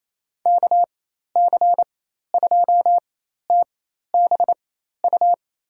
Morse code, heard as a keyed tone, about 28 wpm, spaced as for 13 wpm, 700 Hz. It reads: KC2TBU
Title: USA Call Signs - 6 Characters - 4x Character Spacing - 28wpm